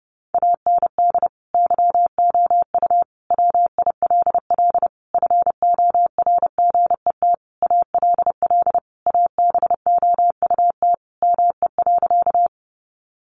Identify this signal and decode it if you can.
Morse code, heard as a keyed tone, about 30 wpm, 700 Hz. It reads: AND YOU WILL FORGET ALL ABOUT ME.
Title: Sentences from Top 1200 Words 30wpm